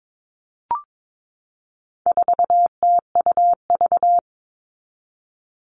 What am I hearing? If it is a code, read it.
Morse code, heard as a keyed tone, about 22 wpm, 700 Hz. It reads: 4TUV
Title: All Cut Numbers - 4-Digit 22wpm